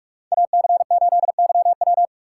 Morse code, heard as a keyed tone, about 48 wpm, 700 Hz. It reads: AC8YW